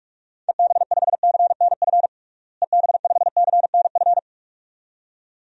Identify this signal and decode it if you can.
Morse code, heard as a keyed tone, about 47 wpm, 700 Hz. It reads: EB5CNF EB5CNF
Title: DXCC-100 Call Signs - 6 Characters - Repeated 2x - 45wpm